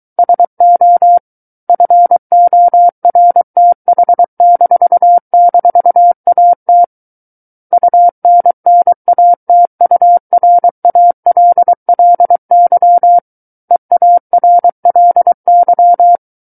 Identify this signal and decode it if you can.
Morse code, heard as a keyed tone, about 23 wpm, 700 Hz. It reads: SO FORTH--AT UNNATURALLY EARLY
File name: Book One / 14-Chapter 14 - 23 WPM